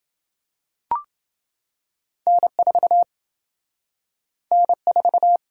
Morse code, heard as a keyed tone, about 27 wpm, 700 Hz. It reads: N4 N4